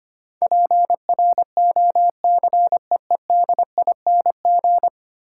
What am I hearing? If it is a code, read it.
Morse code, heard as a keyed tone, about 25 wpm, 700 Hz. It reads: PROCEEDING
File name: 10 Letter Words - First 3 Letters Given - 25wpm